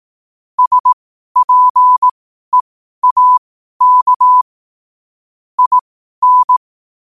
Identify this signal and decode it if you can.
Morse code, heard as a keyed tone, about 18 wpm, spaced as for 12 wpm, 1000 Hz. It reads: SPEAK IN
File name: qotdfiles / 12wpm